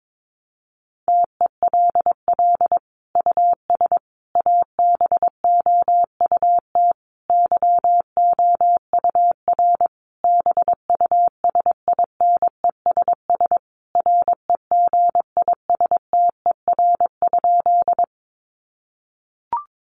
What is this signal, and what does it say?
Morse code, heard as a keyed tone, about 22 wpm, 700 Hz. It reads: TELL US ABOUT YOUR BUSINESS REGISTER?
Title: Sentences from Top 1200 Words 22wpm